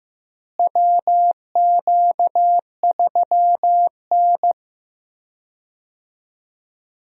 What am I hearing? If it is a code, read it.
Morse code, heard as a keyed tone, about 15 wpm, 700 Hz. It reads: WQ3N
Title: USA Call Signs - 4 Characters - Repeated 3x - 15wpm